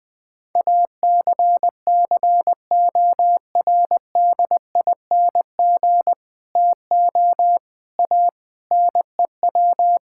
Morse code, heard as a keyed tone, about 20 wpm, 700 Hz. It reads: ACCORDING TO A NEW
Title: Sentences with 7-Word Phrases - 20wpm